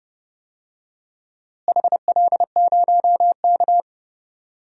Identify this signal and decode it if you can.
Morse code, heard as a keyed tone, about 30 wpm, 700 Hz. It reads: HL0K
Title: DXCC-100 Call Signs - 4 Characters - Repeated 3x - 30wpm